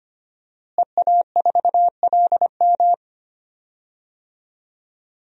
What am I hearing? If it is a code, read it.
Morse code, heard as a keyed tone, about 25 wpm, 700 Hz. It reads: EA4LM
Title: DXCC-100 Call Signs - 5 Characters - Repeated 3x - 25wpm